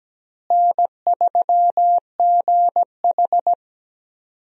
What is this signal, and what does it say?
Morse code, heard as a keyed tone, about 17 wpm, 700 Hz. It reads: N3GH
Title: USA Call Signs - 4 Characters - Repeated 3x - 17wpm